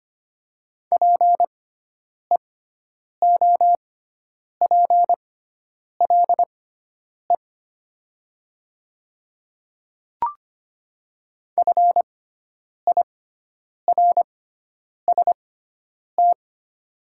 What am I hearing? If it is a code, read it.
Morse code, heard as a keyed tone, about 25 wpm, spaced as for 9 wpm, 700 Hz. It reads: PEOPLE FIRST
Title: 6x Character Spacing - Top 200 Words - 25wpm